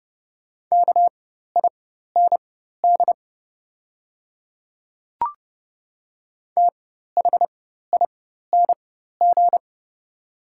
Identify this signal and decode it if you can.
Morse code, heard as a keyed tone, about 30 wpm, spaced as for 14 wpm, 700 Hz. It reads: KIND THING
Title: 4x Character Spacing - Top 500 Words - 30wpm